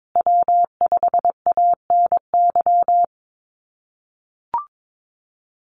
Morse code, heard as a keyed tone, about 22 wpm, 700 Hz. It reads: W5ANY
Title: USA Call Signs Repeated 3x 22wpm